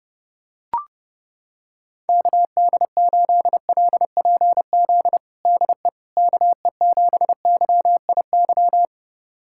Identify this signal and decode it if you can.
Morse code, heard as a keyed tone, about 32 wpm, 700 Hz. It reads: KD8LPZ DE KE7YIY
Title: USA Callsign DE Callsign - 6 Characters - 30wpm